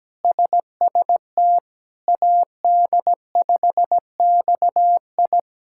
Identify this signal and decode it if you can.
Morse code, heard as a keyed tone, about 17 wpm, 700 Hz. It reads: SST AD5XI
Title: K1USN SST QSO Practice - 17wpm